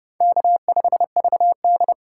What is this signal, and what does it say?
Morse code, heard as a keyed tone, about 30 wpm, 700 Hz. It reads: K5VD
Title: USA Call Signs - 4 Characters - 4x Character Spacing - 30wpm